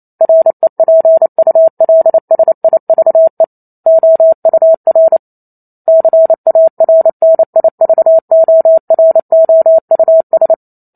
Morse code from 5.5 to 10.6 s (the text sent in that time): CARNIVOROUS